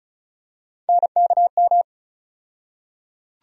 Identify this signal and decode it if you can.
Morse code, heard as a keyed tone, about 36 wpm, 700 Hz. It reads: NKM
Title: USA Call Sign Postfixes - 3 Letters 35wpm